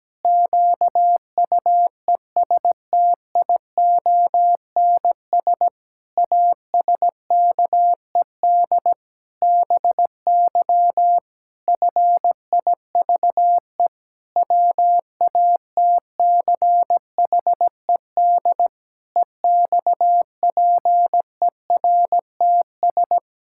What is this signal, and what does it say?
Morse code, heard as a keyed tone, about 17 wpm, 700 Hz. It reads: QUESTIONS ASKED BY FIVE WATCHED EXPERTS
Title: Pangram Challenge v2 - 17wpm